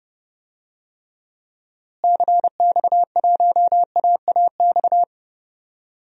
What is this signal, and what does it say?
Morse code, heard as a keyed tone, about 32 wpm, 700 Hz. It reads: CX1AAX
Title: DXCC-100 Call Signs - 6 Characters - Repeated 3x - 30wpm